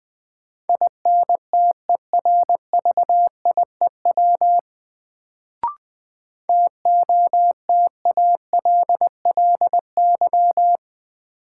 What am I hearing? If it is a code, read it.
Morse code, heard as a keyed tone, about 20 wpm, 700 Hz. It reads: INTERVIEW TOTALLY
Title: Sentences from Top 1600 Words - New Word Drill - 20wpm